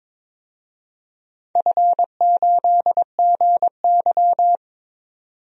Morse code, heard as a keyed tone, about 21 wpm, 700 Hz. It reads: F8GY
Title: DXCC-100 Call Signs - 4 Characters - 22wpm